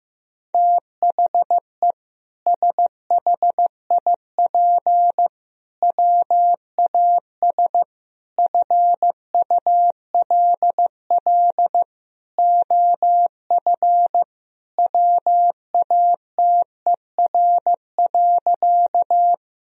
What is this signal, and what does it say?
Morse code, heard as a keyed tone, about 15 wpm, 700 Hz. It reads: THE SHIP WAS FULL OF WATER.
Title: Sentences from Top 1000 Words - Review Part 3 - 15wpm